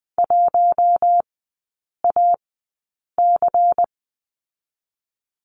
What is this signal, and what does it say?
Morse code, heard as a keyed tone, about 20 wpm, spaced as for 8 wpm, 700 Hz. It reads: 1AC
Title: 100 Lesson 17e - Letter u - Three Characters 20wpm